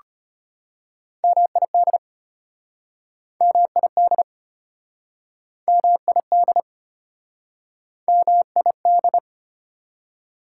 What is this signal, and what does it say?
Morse code, heard as a keyed tone, about 33 wpm, 700 Hz. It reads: MID MID MID MID